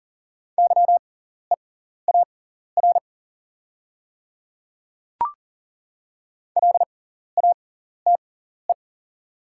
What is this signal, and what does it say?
Morse code, heard as a keyed tone, about 40 wpm, spaced as for 14 wpm, 690 Hz. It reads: YEAR LATE
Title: 6x Character Spacing - Top 500 Words - 40wpm